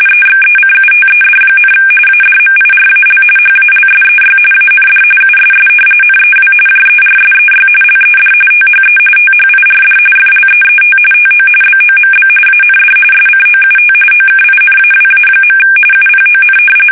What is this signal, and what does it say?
RTTY, about 45.45 baud. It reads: EXECUTED, ADMIRAL HALSEY AND USS NEW JERSEY LED THE THIRD FLEET TO BOMB AND SCOUT OKINAWA. UPON
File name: fsk850 / battleshipnj